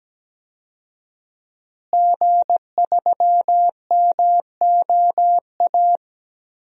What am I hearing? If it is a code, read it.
Morse code, heard as a keyed tone, about 17 wpm, 700 Hz. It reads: G3MOA
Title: DXCC-100 Call Signs - 5 Characters - Repeated 3x - 17wpm